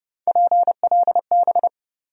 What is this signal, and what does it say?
Morse code, heard as a keyed tone, about 30 wpm, 700 Hz. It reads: PLB
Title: USA Call Sign Postfixes - 3 Letters 30wpm